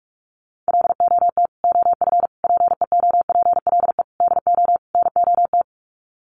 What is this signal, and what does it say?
Morse code, heard as a keyed tone, about 44 wpm, 700 Hz. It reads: LOT OF PEOPLE DO NOT
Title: Sets of 5 Words Encore 45wpm